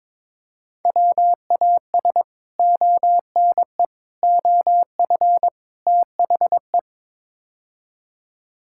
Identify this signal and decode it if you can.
Morse code, heard as a keyed tone, about 22 wpm, 700 Hz. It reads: WAS ONE OF THE
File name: practice / 22wpm / Build A Phrase - 5-Word Phrases - 22wpm